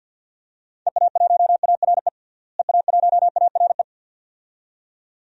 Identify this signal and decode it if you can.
Morse code, heard as a keyed tone, about 47 wpm, 700 Hz. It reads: EA1ARE EA1ARE
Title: DXCC-100 Call Signs - 6 Characters - Repeated 2x - 50wpm